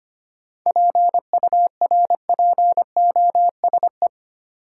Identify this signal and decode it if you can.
Morse code, heard as a keyed tone, about 25 wpm, 700 Hz. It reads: PURPOSE
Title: Sentences from Top 800 Words - New Word Drill - 25wpm